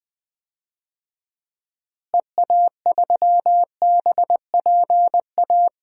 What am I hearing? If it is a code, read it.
Morse code, heard as a keyed tone, about 20 wpm, 690 Hz. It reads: EA3BPA
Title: DXCC-100 Call Signs - 6 Characters - Repeated 2x - 20wpm